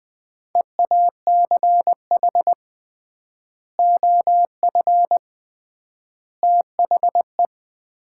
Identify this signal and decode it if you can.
Morse code, heard as a keyed tone, about 20 wpm, 700 Hz. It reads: EACH OF THE
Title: 3x Word-Spacing - Sets of 3 Words - 20wpm